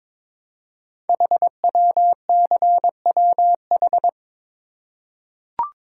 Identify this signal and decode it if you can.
Morse code, heard as a keyed tone, about 22 wpm, 700 Hz. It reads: HWCWH
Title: Code Groups - 5 Letters - 22wpm